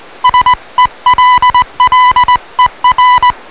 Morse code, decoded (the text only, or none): SELLER